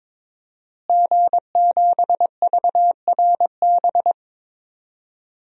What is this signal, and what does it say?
Morse code, heard as a keyed tone, about 22 wpm, 700 Hz. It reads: G7VRB
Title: DXCC-100 Call Signs - 5 Characters - Repeated 3x - 22wpm